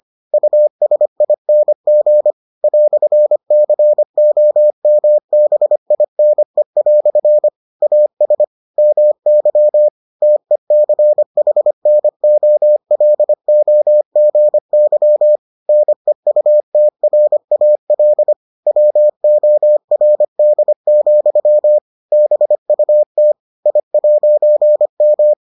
Morse code, heard as a keyed tone, about 25 wpm, 600 Hz. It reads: USING "COMBINE" AS MY TECHNOLOGY NEUTRAL WORD, BUT I'M